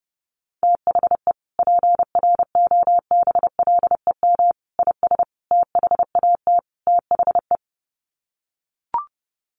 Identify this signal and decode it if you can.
Morse code, heard as a keyed tone, about 30 wpm, 700 Hz. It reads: THE PROBLEM IS THAT THE
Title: Sets of 5 Words Encore2 30wpm